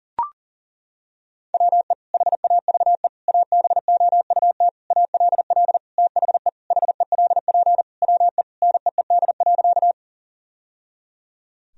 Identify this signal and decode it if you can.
Morse code, heard as a keyed tone, about 40 wpm, 700 Hz. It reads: WE HAVE ABOUT ALL THE HELP WE NEED.
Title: Sentences from Top 1000 Words - Review Part 1 - 40wpm